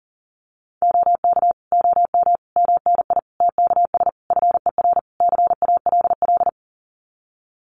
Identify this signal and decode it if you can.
Morse code, heard as a keyed tone, about 40 wpm, 700 Hz. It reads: OK OM MNI TKS FER CALL
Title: Common QSO Phrases 40wpm